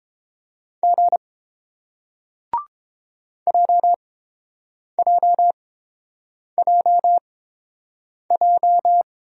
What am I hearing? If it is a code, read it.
Morse code, heard as a keyed tone, about 29 wpm, spaced as for 7 wpm, 700 Hz. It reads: G JJJJ